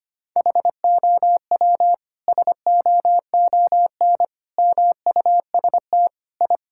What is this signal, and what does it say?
Morse code, heard as a keyed tone, about 25 wpm, 700 Hz. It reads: HOW SOON MUST I
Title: Sentences from Top 1000 Words - Review Part 5 - 25wpm